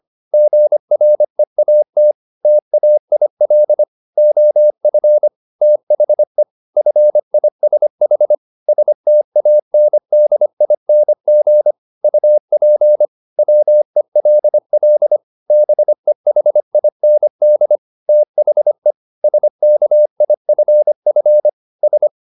Morse code, heard as a keyed tone, about 25 wpm, 600 Hz. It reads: GREAT TAIL OF THE FISH STANDING UP WELL BEHIND THE SKIFF S